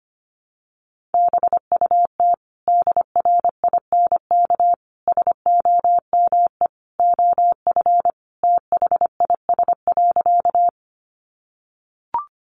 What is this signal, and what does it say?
Morse code, heard as a keyed tone, about 25 wpm, 700 Hz. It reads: BUT DRINK SOME OF THIS.
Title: Sentences from Top 1500 Words 25wpm